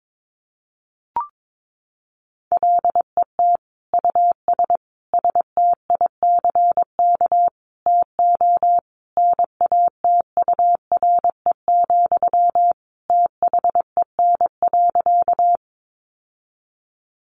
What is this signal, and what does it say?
Morse code, heard as a keyed tone, about 22 wpm, 700 Hz. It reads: LET US STICK TO NATURE, THEN.
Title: Sentences from Top 1200 Words 22wpm